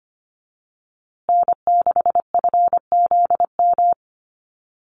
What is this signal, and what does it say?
Morse code, heard as a keyed tone, about 25 wpm, 700 Hz. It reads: N6FZM
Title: USA Call Signs Repeated 2x 25wpm